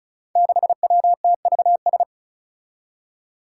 Morse code, heard as a keyed tone, about 35 wpm, 700 Hz. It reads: 6WTVS
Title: Five Challenging Letter-Number Combinations 35wpm